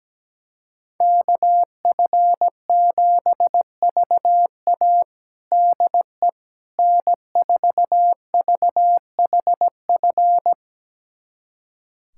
Morse code, heard as a keyed tone, about 17 wpm, 700 Hz. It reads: KF7VA DE N4VHF
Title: USA Callsign DE Callsign - 5 Characters - 17wpm